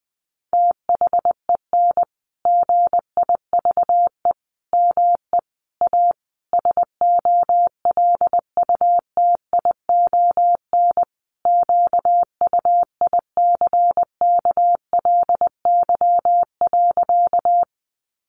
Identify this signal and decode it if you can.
Morse code, heard as a keyed tone, about 20 wpm, 700 Hz. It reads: THEN GIVE ME A SOLUTION QUICKLY.
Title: Sentences from Top 1300 Words 20wpm